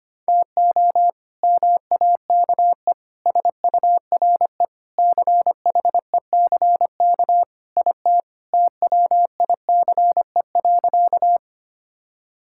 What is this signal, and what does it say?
Morse code, heard as a keyed tone, about 25 wpm, 700 Hz. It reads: TO MAKE SURE CHECK IT TWICE.
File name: practice / 25wpm / Sentences with 3-Word Phrases - 25wpm